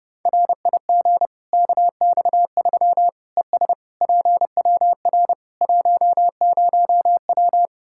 Morse code, heard as a keyed tone, about 30 wpm, 700 Hz. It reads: RIG KX3 ES PWR 10W
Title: Practice QSO - Exchange 2 - Info - 30wpm